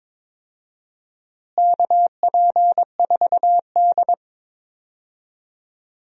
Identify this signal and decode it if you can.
Morse code, heard as a keyed tone, about 22 wpm, 700 Hz. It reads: KP4D
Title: DXCC-100 Call Signs - 4 Characters - Repeated 2x - 22wpm